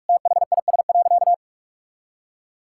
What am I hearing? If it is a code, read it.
Morse code, heard as a keyed tone, about 46 wpm, 700 Hz. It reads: THIS.
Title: Sentences from Top 1000 Words - Review Part 6 - 45wpm